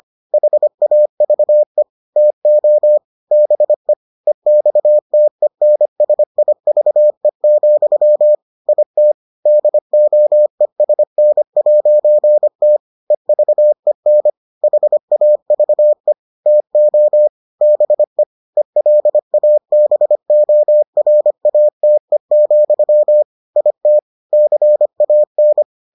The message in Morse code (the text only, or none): HAVE TO BE EXTENSIVE, IT DOESN'T EVEN HAVE TO BE ELABORATE, IT CAN